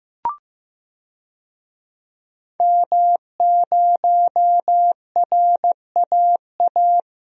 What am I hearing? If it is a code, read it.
Morse code, heard as a keyed tone, about 15 wpm, 700 Hz. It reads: M0RAA
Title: DXCC-100 Call Signs - 5 Characters - 15wpm